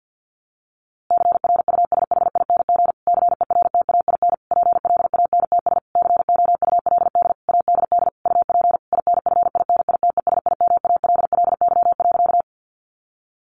Morse code, heard as a keyed tone, about 50 wpm, 700 Hz. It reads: CRUSHING CERTAIN PLANTS COULD ADD UP INFINITESIMALLY.
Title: A Sound Of Thunder 50wpm